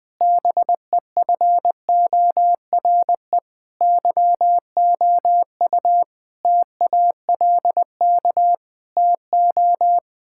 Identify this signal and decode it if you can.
Morse code, heard as a keyed tone, about 20 wpm, 700 Hz. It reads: BEFORE YOU TALK TO
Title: Sentences from Top 1000 Words - Review Part 6 - 20wpm